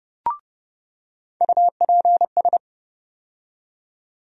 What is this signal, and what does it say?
Morse code, heard as a keyed tone, about 30 wpm, 700 Hz. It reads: UPS
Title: USA Call Sign Postfixes - 3 Letters 30wpm